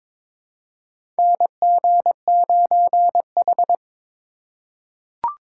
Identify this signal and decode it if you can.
Morse code, heard as a keyed tone, about 21 wpm, 700 Hz. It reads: NG9H
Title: USA Call Signs - 3x Character Spacing - 22wpm